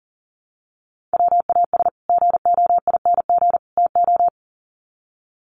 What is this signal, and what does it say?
Morse code, heard as a keyed tone, about 40 wpm, 700 Hz. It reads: WAS GOING TO
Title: Sets of 3 Words 40wpm